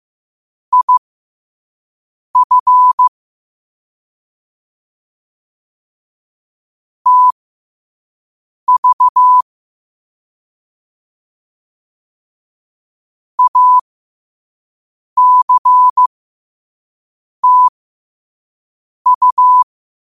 Morse code, heard as a keyed tone, about 15 wpm, spaced as for 5 wpm, 1000 Hz. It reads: IF TV ACTU
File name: qotdfiles / 5wpm